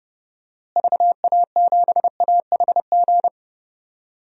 Morse code, heard as a keyed tone, about 30 wpm, 700 Hz. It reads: VA7AHG
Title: DXCC-100 Call Signs - 6 Characters - Repeated 2x - 30wpm